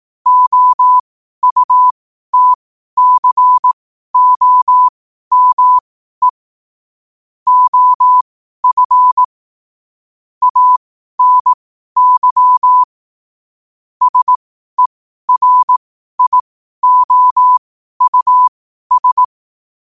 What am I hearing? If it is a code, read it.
Morse code, heard as a keyed tone, about 18 wpm, spaced as for 12 wpm, 1000 Hz. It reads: OUTCOME OF ANY SERIOUS
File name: qotdfiles / 12wpm